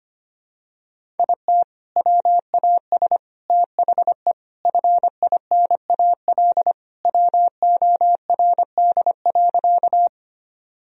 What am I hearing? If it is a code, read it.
Morse code, heard as a keyed tone, about 25 wpm, 700 Hz. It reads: IT WAS THE FINAL WORD.
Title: Sentences from Top 1000 Words - Review Part 2 - 25wpm